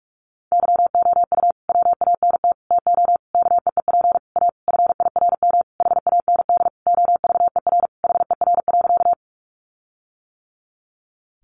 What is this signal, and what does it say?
Morse code, heard as a keyed tone, about 44 wpm, 700 Hz. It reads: YOU WANT TO KEEP A FIRM HAND OVER HER.
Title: Sentences from Top 1400 Words 45wpm